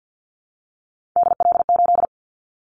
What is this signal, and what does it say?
Morse code, heard as a keyed tone, about 50 wpm, 700 Hz. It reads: DL8
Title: Callsign Prefixes for Top 25 DXCC - Repeated 3x - 50wpm